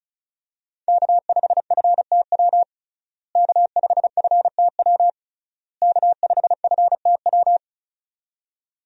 Morse code, heard as a keyed tone, about 36 wpm, 700 Hz. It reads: K5FTW K5FTW K5FTW
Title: USA Call Signs - 5 Characters - Repeated 3x - 35wpm